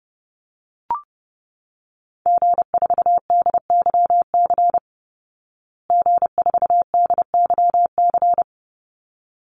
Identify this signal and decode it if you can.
Morse code, heard as a keyed tone, about 30 wpm, 700 Hz. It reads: G4DYC G4DYC
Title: Call Signs Repeated 3x - 30wpm